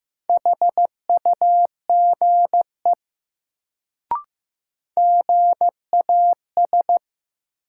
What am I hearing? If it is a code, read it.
Morse code, heard as a keyed tone, about 15 wpm, 700 Hz. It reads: HUGE GAS
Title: Sentences from Top 1100 Words - New Word Drill Alt - 15wpm